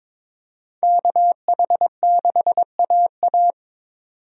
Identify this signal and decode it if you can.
Morse code, heard as a keyed tone, about 22 wpm, 700 Hz. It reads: KH6AA
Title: DXCC-100 Call Signs - 5 Characters - 22wpm